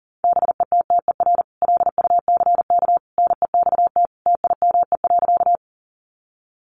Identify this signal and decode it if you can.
Morse code, heard as a keyed tone, about 40 wpm, 700 Hz. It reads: BETTER LUCK NEXT TIME.
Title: Sentences from Top 1500 Words 40wpm